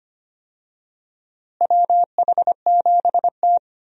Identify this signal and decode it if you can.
Morse code, heard as a keyed tone, about 25 wpm, 700 Hz. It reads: WH7T
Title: USA Call Signs - 4 Characters - Repeated 2x - 25wpm